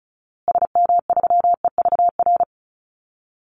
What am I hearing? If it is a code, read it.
Morse code, heard as a keyed tone, about 34 wpm, 700 Hz. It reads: SM3EVR
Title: Call Signs Repeated 2x - 35wpm